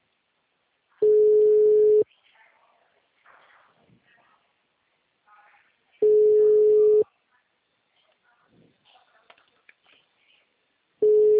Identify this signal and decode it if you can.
Ringback tone (Europe, 425 Hz), with speech.